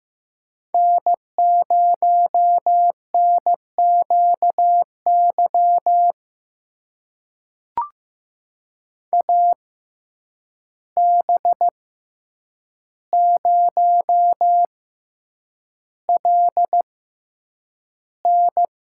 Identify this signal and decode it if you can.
Morse code, heard as a keyed tone, about 15 wpm, spaced as for 8 wpm, 700 Hz. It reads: N0NQY AB0LN
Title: USA Call Signs - 5 Characters - 6x Character Spacing - 15wpm